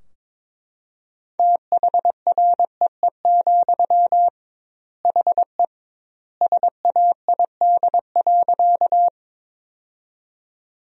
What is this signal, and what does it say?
Morse code, heard as a keyed tone, about 22 wpm, 700 Hz. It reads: THREE, HE SAID.
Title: 2x Word-Spacing - Sentences from Top 200 Words - 22wpm